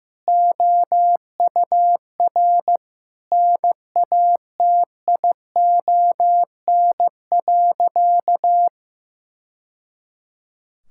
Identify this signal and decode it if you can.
Morse code, heard as a keyed tone, about 15 wpm, 700 Hz. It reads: OUR NATION.
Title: Sentences from Top 1000 Words - Review Part 6 - 15wpm